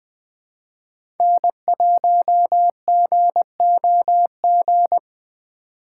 Morse code, heard as a keyed tone, about 21 wpm, 700 Hz. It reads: N1GOG
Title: USA Call Signs - 3x Character Spacing - 20wpm